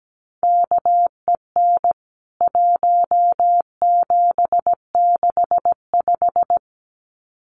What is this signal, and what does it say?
Morse code, heard as a keyed tone, about 17 wpm, 700 Hz. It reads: KEN 1765
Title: CWT Exchange 17wpm